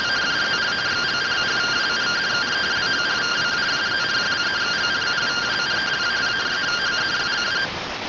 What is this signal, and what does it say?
Two-tone teleprinter signal, about 170 Hz shift.